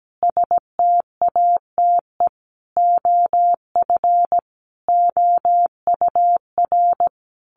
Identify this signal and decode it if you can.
Morse code, heard as a keyed tone, about 17 wpm, 700 Hz. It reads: STATE OF OUR
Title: Sentences from Top 1400 Words 17wpm